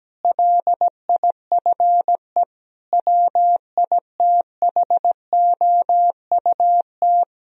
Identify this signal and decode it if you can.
Morse code, heard as a keyed tone, about 17 wpm, 700 Hz. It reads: LIFE WITHOUT
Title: Sentences from Top 1000 Words - Review Part 3 - 17wpm